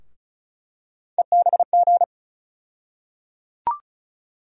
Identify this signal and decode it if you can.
Morse code, heard as a keyed tone, about 36 wpm, 700 Hz. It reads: EBG
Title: USA Call Sign Postfixes - 3 Letters 35wpm